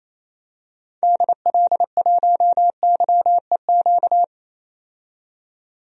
Morse code, heard as a keyed tone, about 28 wpm, 700 Hz. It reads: DL1YEQ
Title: DXCC-100 Call Signs - 6 Characters - 28wpm